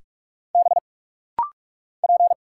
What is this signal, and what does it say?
Morse code, heard as a keyed tone, about 48 wpm, 690 Hz. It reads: B P
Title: Single Letter-Number - Rapid-Fire 45wpm